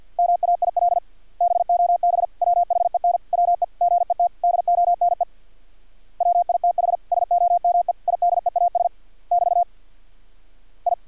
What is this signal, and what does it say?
Morse code, heard as a keyed tone, about 50 wpm, 700 Hz. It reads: MAIL BOB WHEN WE GET DONE WITH SOME IDEAS <BT> I
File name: archived / CRSnet